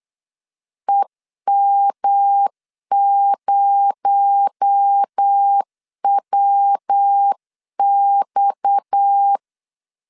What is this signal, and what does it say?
Morse code, heard as a keyed tone, about 8 wpm, 800 Hz. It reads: EM0WX
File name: LEZIONI / 7 GF9Z / 7 GF 8wpm 3108